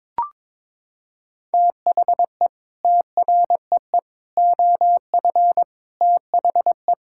Morse code, heard as a keyed tone, about 22 wpm, 700 Hz. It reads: THE TREE OF THE
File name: Sets of 9 Words - 22wpm